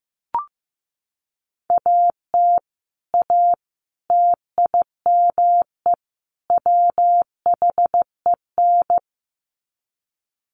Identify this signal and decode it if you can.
Morse code, heard as a keyed tone, about 15 wpm, 700 Hz. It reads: AT A TIME WHEN
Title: Sets of 4 Words 15wpm